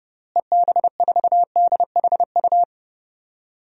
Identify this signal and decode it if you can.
Morse code, heard as a keyed tone, about 30 wpm, 700 Hz. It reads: EB4DHU
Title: DXCC-100 Call Signs - 6 Characters - Repeated 2x - 30wpm